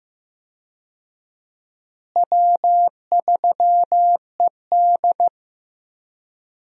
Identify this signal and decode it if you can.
Morse code, heard as a keyed tone, about 15 wpm, 700 Hz. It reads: W3ED
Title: USA Call Signs - 4 Characters - Repeated 2x - 15wpm